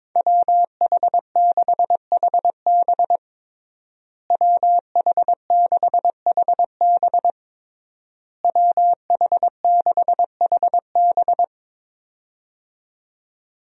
Morse code, heard as a keyed tone, about 22 wpm, 700 Hz. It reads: WH6HB WH6HB WH6HB
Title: USA Call Signs - 5 Characters - Repeated 3x - 22wpm